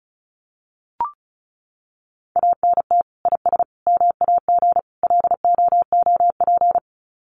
Morse code, heard as a keyed tone, about 34 wpm, 700 Hz. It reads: ANT IS MAG LOOP
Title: Common QSO Phrases 35wpm